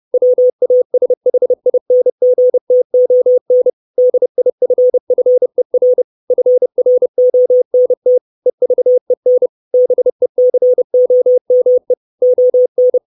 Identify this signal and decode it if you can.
Morse code, heard as a keyed tone, about 30 wpm, 500 Hz. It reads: WASHINGTON DIFFER FRONT EVEN BECOME ON
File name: 632 Words 30WPM